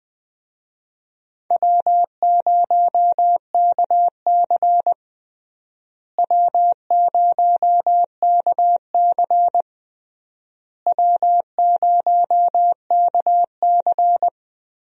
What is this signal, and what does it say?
Morse code, heard as a keyed tone, about 21 wpm, 700 Hz. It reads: W0KC W0KC W0KC
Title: USA Call Signs - 4 Characters - Repeated 3x - 20wpm